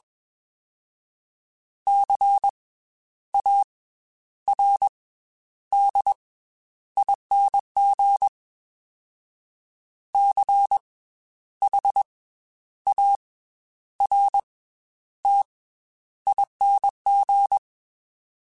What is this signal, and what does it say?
Morse code, heard as a keyed tone, about 21 wpm, 800 Hz. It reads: C A R D ING C H A R T ING
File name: Special INGs / Special INGs-mp3s / Slow ing @20